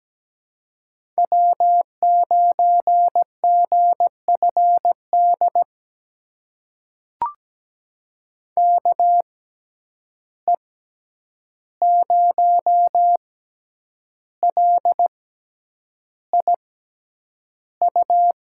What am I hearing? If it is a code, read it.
Morse code, heard as a keyed tone, about 17 wpm, spaced as for 8 wpm, 700 Hz. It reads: W9GFD KE0LIU
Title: USA Call Signs - 6x Character Spacing - 17wpm